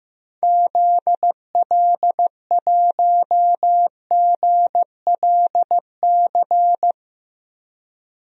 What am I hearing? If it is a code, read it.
Morse code, heard as a keyed tone, about 15 wpm, 700 Hz. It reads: ZL1GLC
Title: DXCC-100 Call Signs - 6 Characters - 15wpm